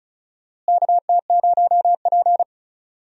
Morse code, heard as a keyed tone, about 34 wpm, 700 Hz. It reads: KT0P
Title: USA Call Signs - 4 Characters - Repeated 3x - 35wpm